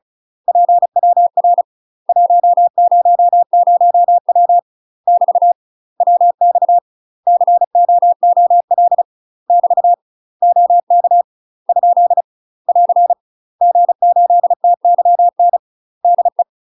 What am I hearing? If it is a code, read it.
Morse code, heard as a keyed tone, about 34 wpm, 700 Hz. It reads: PWR 100W <BT> WX COOL <BT> OK ? <AR> G8TYN DE
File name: UK MASTER QSOs TRAINING/35WPM